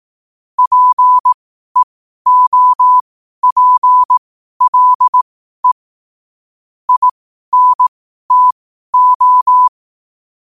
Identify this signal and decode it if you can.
Morse code, heard as a keyed tone, about 18 wpm, spaced as for 12 wpm, 1000 Hz. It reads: PEOPLE INTO
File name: qotdfiles / 12wpm